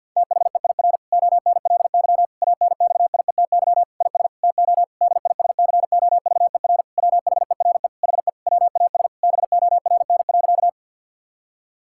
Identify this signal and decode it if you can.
Morse code, heard as a keyed tone, about 50 wpm, 700 Hz. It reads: THEIR ONLY ANXIETY IS TO DISCOVER WHERE HE WAS BORN.
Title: Sentences from Top 1600 Words 50wpm